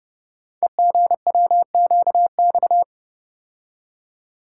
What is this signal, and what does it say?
Morse code, heard as a keyed tone, about 30 wpm, 700 Hz. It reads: EGWQX
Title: Code Groups - 5 Letters - 30wpm